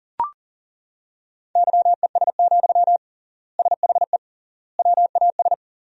Morse code, heard as a keyed tone, about 40 wpm, 700 Hz. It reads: YES, SHE WAS
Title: 3x Word-Spacing - Sentences from Top 500 Words - 40wpm